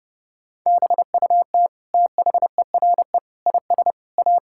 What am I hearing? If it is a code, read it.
Morse code, heard as a keyed tone, about 30 wpm, 700 Hz. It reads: BUT THERE IS A
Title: Sentences from Top 1000 Words - Review Part 6 - 30wpm